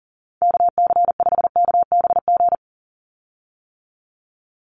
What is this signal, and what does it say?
Morse code, heard as a keyed tone, about 40 wpm, 690 Hz. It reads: KC5KBG
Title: USA Call Signs 40wpm